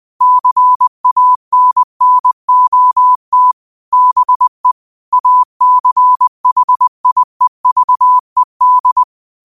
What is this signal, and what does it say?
Morse code, heard as a keyed tone, about 20 wpm, 1000 Hz. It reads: CANNOT BE ACHIEVED
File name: qotdfiles / 20wpm